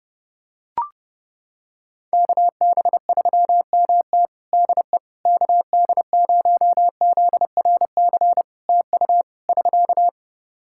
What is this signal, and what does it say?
Morse code, heard as a keyed tone, about 32 wpm, 700 Hz. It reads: KB3MT DE KD0ZRC TU <SK>
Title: Practice QSO - Exchange 4 - Ending - 30wpm